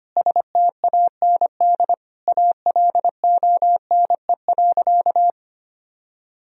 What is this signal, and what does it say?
Morse code, heard as a keyed tone, about 25 wpm, 700 Hz. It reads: STAND ALONE.
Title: Sentences from Top 1000 Words - Review Part 3 - 25wpm